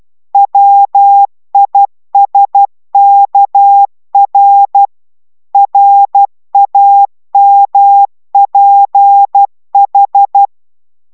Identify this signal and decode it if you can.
Morse code, heard as a keyed tone, about 12 wpm, 800 Hz. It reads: WISKR RAMPH